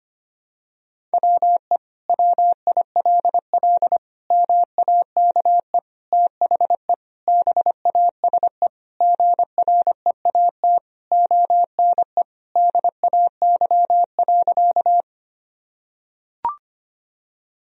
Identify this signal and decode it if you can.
Morse code, heard as a keyed tone, about 25 wpm, 700 Hz. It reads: WE WILL MAKE THE BASE GREAT ONE DAY.
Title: Sentences from Top 1000 Words - Review Part 1 - 25wpm